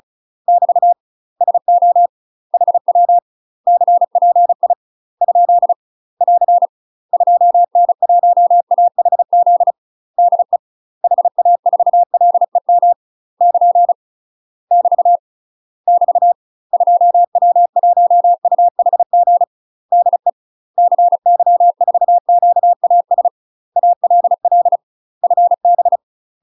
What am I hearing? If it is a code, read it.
Morse code, heard as a keyed tone, about 35 wpm, 700 Hz. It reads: <BT> SO HW CPI ? <AR> 2N1AHZ DE HA4LEM <KN> <BT> <BT> 2W1UHG DE CY4QAS ALL FB
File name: UK MASTER QSOs TRAINING / 35WPM / UK MASTER QSOs 0042